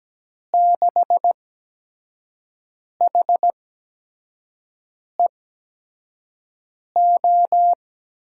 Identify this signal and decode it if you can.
Morse code, heard as a keyed tone, about 17 wpm, spaced as for 5 wpm, 700 Hz. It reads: 6HEO
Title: USA Call Signs - 8x Character Spacing - 17wpm